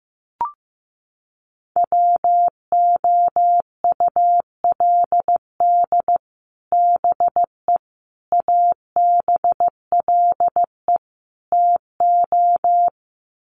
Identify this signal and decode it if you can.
Morse code, heard as a keyed tone, about 15 wpm, 700 Hz. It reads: WOULD BE ABLE TO
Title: Sets of 4 Words 15wpm